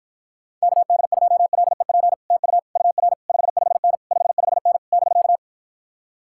Morse code, heard as a keyed tone, about 52 wpm, 700 Hz. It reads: KD2LEP TU UR 55N 55N <BK>